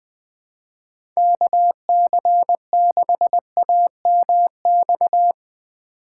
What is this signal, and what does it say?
Morse code, heard as a keyed tone, about 20 wpm, 700 Hz. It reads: KC6AMX
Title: USA Call Signs - 6 Characters - 3x Character Spacing - 20wpm